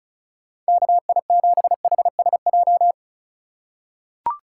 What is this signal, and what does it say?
Morse code, heard as a keyed tone, about 36 wpm, 700 Hz. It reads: KI7HSJ
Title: USA Call Signs - 6 Characters - 2x Character Spacing - 35wpm